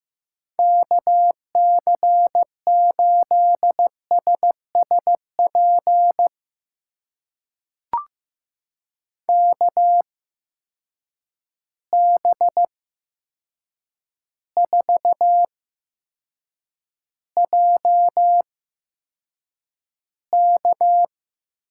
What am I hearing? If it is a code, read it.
Morse code, heard as a keyed tone, about 15 wpm, spaced as for 7 wpm, 700 Hz. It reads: KC8SSP KB4JK
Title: USA Call Signs - 6 Characters - 8x Character Spacing - 15wpm